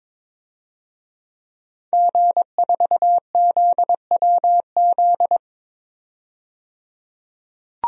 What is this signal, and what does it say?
Morse code, heard as a keyed tone, about 21 wpm, 700 Hz. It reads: G4ZWZ
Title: DXCC-100 Call Signs - 5 Characters - 22wpm